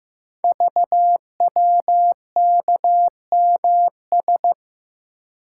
Morse code, heard as a keyed tone, about 15 wpm, 700 Hz. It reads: VWKMS
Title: Code Groups - 5 Letters - 15wpm